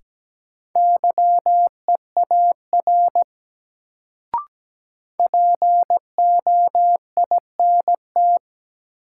Morse code, heard as a keyed tone, about 17 wpm, 700 Hz. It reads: YEAR POINT
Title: Sentences from Top 200 Words - New Word Drill - 17wpm